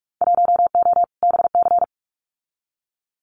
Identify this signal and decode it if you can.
Morse code, heard as a keyed tone, about 46 wpm, 700 Hz. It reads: 1O BC
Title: Winter Field Day Exchanges Encore 45wpm